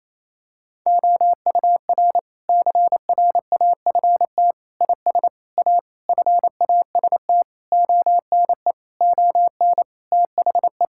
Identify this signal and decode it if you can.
Morse code, heard as a keyed tone, about 28 wpm, 700 Hz. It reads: OUR CRAFT IS A FAST ONE ON THE
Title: Sentences from Top 1000 Words - Review Part 6 - 28wpm